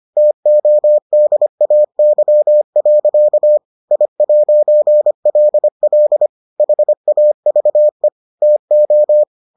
Morse code, heard as a keyed tone, about 25 wpm, 600 Hz. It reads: TODAY. I'LL HAVE TO